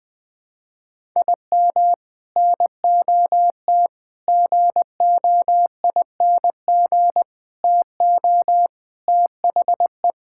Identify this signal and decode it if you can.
Morse code, heard as a keyed tone, about 20 wpm, 700 Hz. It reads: IM NOT GOING TO THE
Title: Sentences with 3-Word Phrases - 20wpm